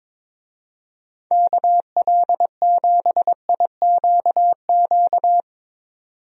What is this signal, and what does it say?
Morse code, heard as a keyed tone, about 22 wpm, 700 Hz. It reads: KL7IQQ
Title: DXCC-100 Call Signs - 6 Characters - 22wpm